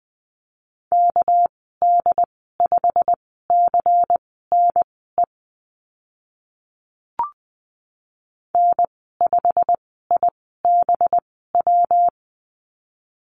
Morse code, heard as a keyed tone, about 20 wpm, spaced as for 15 wpm, 700 Hz. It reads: KD5CNE N5IBW
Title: 200 Lesson 32g - Letter k - Call Signs 20wpm